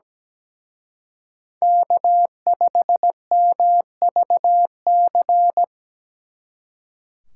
Morse code, heard as a keyed tone, about 20 wpm, 700 Hz. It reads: K5MVC
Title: Speed Racing - USA Call Signs 17wpm